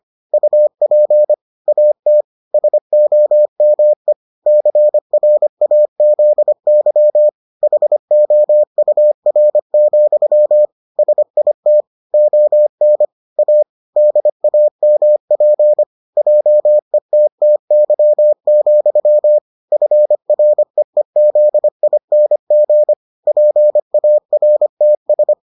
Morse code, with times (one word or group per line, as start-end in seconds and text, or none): 0.3-1.3 UP
1.7-2.2 AT
2.5-4.1 SOME
4.5-7.3 CRAZY
7.6-10.7 HOUR,
11.0-11.8 SIT
12.1-13.1 ON
13.4-13.6 A
14.0-15.8 DAMP
16.2-19.4 JETTY,
19.7-22.9 FREEZING
23.3-25.3 PARTS